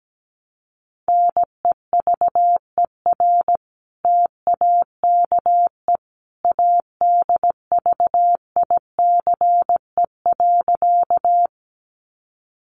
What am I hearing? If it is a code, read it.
Morse code, heard as a keyed tone, about 17 wpm, 700 Hz. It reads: NEVER TAKE ADVICE.
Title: Sentences from Top 1100 Words 17wpm